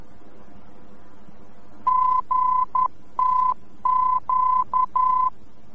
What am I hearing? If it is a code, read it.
Morse code, heard as a keyed tone, about 11 wpm, 1020 Hz. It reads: GTQ
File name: france / mx / GTQ-VOR-F